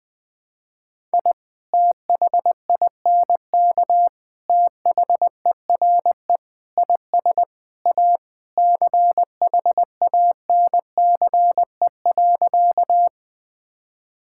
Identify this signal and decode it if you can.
Morse code, heard as a keyed tone, about 20 wpm, 700 Hz. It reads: I THINK THERE IS A CHANCE.
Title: Sentences with 3-Word Phrases - 20wpm